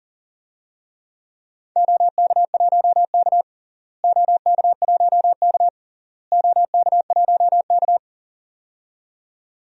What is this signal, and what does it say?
Morse code, heard as a keyed tone, about 40 wpm, 700 Hz. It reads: OK1K OK1K OK1K
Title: DXCC-100 Call Signs - 4 Characters - Repeated 3x - 40wpm